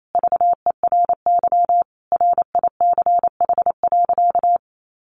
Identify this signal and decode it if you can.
Morse code, heard as a keyed tone, about 28 wpm, 700 Hz. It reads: VERY RICH.
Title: Sentences from Top 1200 Words 28wpm